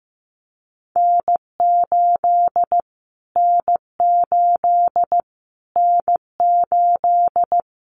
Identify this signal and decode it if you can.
Morse code, heard as a keyed tone, about 15 wpm, 700 Hz. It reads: N8 N8 N8
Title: Callsign Prefixes for Top 25 DXCC - Repeated 3x - 15wpm